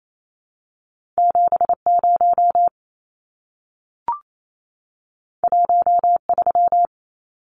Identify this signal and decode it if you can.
Morse code, heard as a keyed tone, about 28 wpm, 700 Hz. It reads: 70 13
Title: Numbers - 2-Digit 28wpm